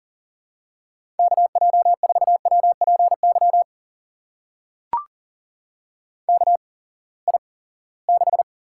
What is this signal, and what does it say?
Morse code, heard as a keyed tone, about 40 wpm, spaced as for 23 wpm, 700 Hz. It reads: KJ4WPY KI6
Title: USA Call Signs - 6 Characters - 8x Character Spacing - 40wpm